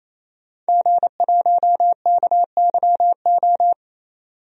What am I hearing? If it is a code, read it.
Morse code, heard as a keyed tone, about 28 wpm, 700 Hz. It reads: G1KYO
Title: DXCC-100 Call Signs - 5 Characters - 28wpm